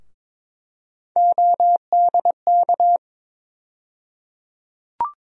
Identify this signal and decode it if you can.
Morse code, heard as a keyed tone, about 21 wpm, 700 Hz. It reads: ODK
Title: USA Call Sign Postfixes - 3 Letters 22wpm